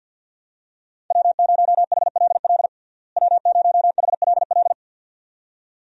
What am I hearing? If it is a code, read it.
Morse code, heard as a keyed tone, about 48 wpm, 700 Hz. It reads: W0HLL W0HLL
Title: USA Call Signs - 5 Characters - Repeated 2x - 50wpm